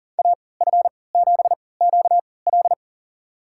Morse code, heard as a keyed tone, about 37 wpm, spaced as for 22 wpm, 700 Hz. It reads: AF7QL
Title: USA Call Signs - 5 Characters - 3x Character Spacing - 40wpm